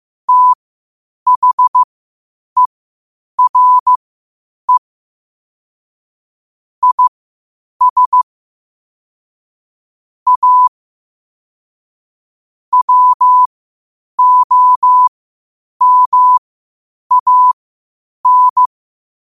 Morse code, heard as a keyed tone, about 15 wpm, spaced as for 8 wpm, 1000 Hz. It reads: THERE IS A WOMAN
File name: qotdfiles / 8wpm